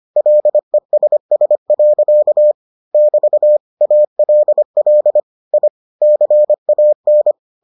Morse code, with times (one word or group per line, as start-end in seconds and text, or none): 0.2-2.5 LESS.
2.9-3.6 <BT>
3.8-5.2 ALL
5.5-5.7 I
6.0-7.3 CAN